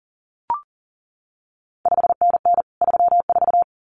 Morse code, heard as a keyed tone, about 39 wpm, 700 Hz. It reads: 5NN 34
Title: CQ WW DX Contest Exchange Encore - 40wpm